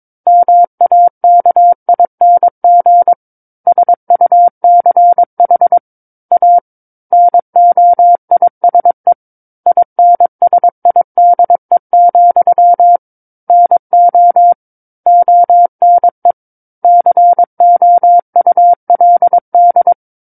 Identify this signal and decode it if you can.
Morse code, heard as a keyed tone, about 22 wpm, 700 Hz. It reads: MAKING SUCH A NOISE INSIDE, NO ONE COULD
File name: 06-Chapter-06 22 WPM